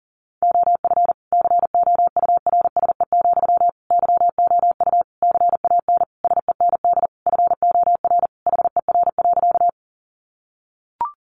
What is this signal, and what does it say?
Morse code, heard as a keyed tone, about 40 wpm, 700 Hz. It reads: OF COURSE, YOU CAN SEND FOR HER.
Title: Sentences from Top 1500 Words 40wpm